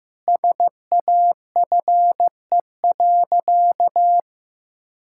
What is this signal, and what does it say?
Morse code, heard as a keyed tone, about 15 wpm, 700 Hz. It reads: SAFE.
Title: Sentences from Top 1600 Words 15wpm